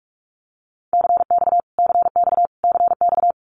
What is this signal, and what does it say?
Morse code, heard as a keyed tone, about 44 wpm, 700 Hz. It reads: CX CX CX
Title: Abbreviations Repeated 3x - 45wpm